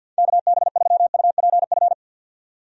Morse code, heard as a keyed tone, about 52 wpm, 700 Hz. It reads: KB3UPF
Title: USA Call Signs - 6 Characters - 50wpm